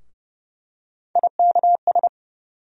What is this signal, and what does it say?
Morse code, heard as a keyed tone, about 32 wpm, 700 Hz. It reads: IKS
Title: USA Call Sign Postfixes - 3 Letters 30wpm